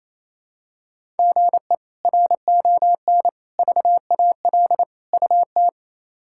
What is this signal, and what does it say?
Morse code, heard as a keyed tone, about 28 wpm, 700 Hz. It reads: GE RON VAL UT
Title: K1USN SST QSO Practice - 28wpm